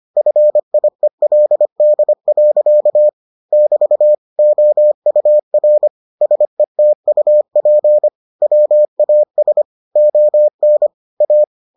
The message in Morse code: FIELD. <BT> OUR SETUP WAS ON A

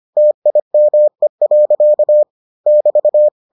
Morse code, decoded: TIME. <BT>

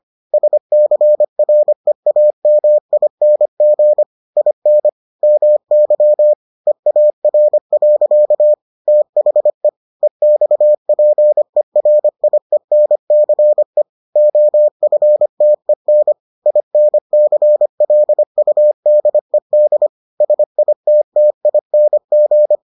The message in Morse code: SCREAMING IN MY EAR. THE EXPERIENCE OFTEN INCLUDED SITTING

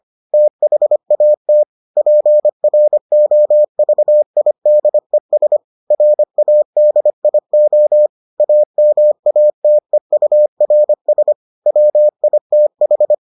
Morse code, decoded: THAT PROVIDES RADIO AMATEURS WITH

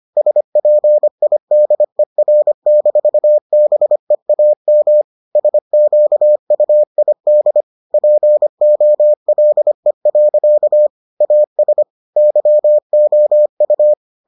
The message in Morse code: SPIDER-BEAM SQUID POLE. AS YOU

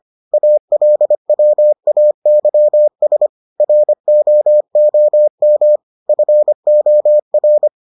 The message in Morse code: ALWAYS ROOM FOR

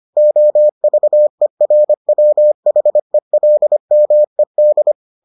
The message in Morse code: OVERWHELMED

